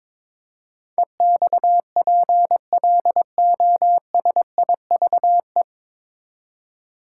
Morse code, heard as a keyed tone, about 21 wpm, 700 Hz. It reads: EXPLOSIVE
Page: Morse Code Ninja practice 22wpm